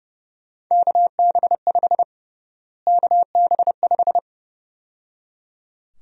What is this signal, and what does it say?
Morse code, heard as a keyed tone, about 30 wpm, 700 Hz. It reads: KB5 KB5
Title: USA Call Sign Prefixes - Repeated 2x - 30wpm